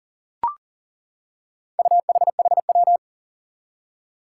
Morse code, heard as a keyed tone, about 40 wpm, 700 Hz. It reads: UHHW
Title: Four Challenging Letter-Number Combinations 40wpm